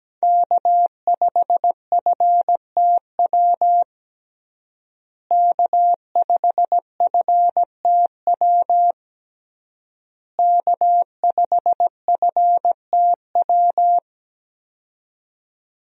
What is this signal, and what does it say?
Morse code, heard as a keyed tone, about 17 wpm, 700 Hz. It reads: K5FTW K5FTW K5FTW
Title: USA Call Signs - 5 Characters - Repeated 3x - 17wpm